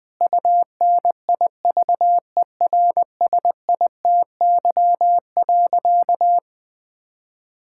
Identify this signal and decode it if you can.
Morse code, heard as a keyed tone, about 20 wpm, 700 Hz. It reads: UNIVERSITY.
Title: Sentences from Top 1000 Words - Review Part 5 - 20wpm